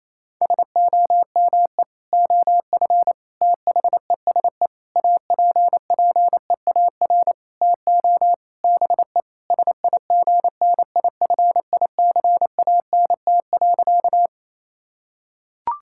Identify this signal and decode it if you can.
Morse code, heard as a keyed tone, about 28 wpm, 700 Hz. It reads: SOME OF THESE APPEAR TO BE SIGNIFICANT.
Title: Sentences from Top 1000 Words - Review Part 6 - 28wpm